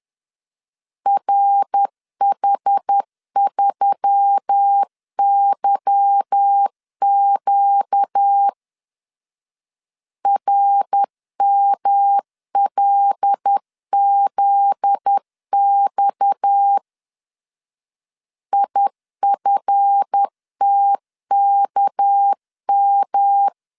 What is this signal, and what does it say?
Morse code, heard as a keyed tone, about 10 wpm, 800 Hz. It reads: RH3YQ RMLZX IFTKM
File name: LEZIONI / 8 YJ45 / 8 YJ4 10wpm 2633